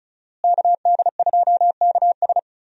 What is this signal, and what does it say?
Morse code, heard as a keyed tone, about 36 wpm, 700 Hz. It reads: KD2KS
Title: USA Call Signs - 5 Characters - Repeated 3x - 35wpm